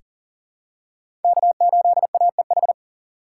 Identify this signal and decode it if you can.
Morse code, heard as a keyed tone, about 40 wpm, 700 Hz. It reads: K8AEH
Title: USA Call Signs - 5 Characters - 3x Character Spacing - 40wpm